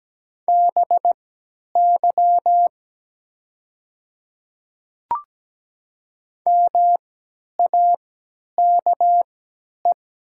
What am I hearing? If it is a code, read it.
Morse code, heard as a keyed tone, about 17 wpm, spaced as for 10 wpm, 700 Hz. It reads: BY MAKE
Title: 3x Character Spacing - Top 100 Words - 17wpm